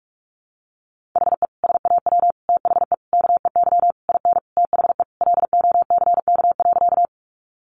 Morse code, heard as a keyed tone, about 44 wpm, 700 Hz. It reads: HE SAW THE KEY IN THE LOCK.
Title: Sentences from Top 1300 Words 45wpm